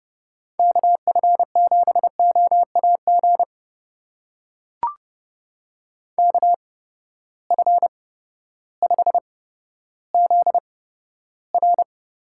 Morse code, heard as a keyed tone, about 30 wpm, spaced as for 14 wpm, 700 Hz. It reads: KF7OAG KF5ZR
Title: USA Call Signs - 6 Characters - 8x Character Spacing - 30wpm